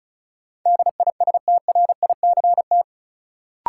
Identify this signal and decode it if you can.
Morse code, heard as a keyed tone, about 36 wpm, 700 Hz. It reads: DISTRICT
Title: Sentences from Top 1600 Words - New Word Drill Alt - 35wpm